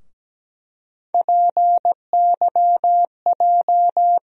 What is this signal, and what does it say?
Morse code, heard as a keyed tone, about 17 wpm, 700 Hz. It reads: PYJ
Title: USA Call Sign Postfixes - 3 Letters 17wpm